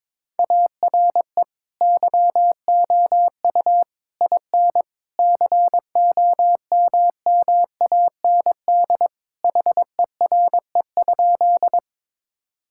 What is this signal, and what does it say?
Morse code, heard as a keyed tone, about 22 wpm, 700 Hz. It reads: ARE YOU IN COMMAND HERE?
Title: Sentences from Top 1600 Words 22wpm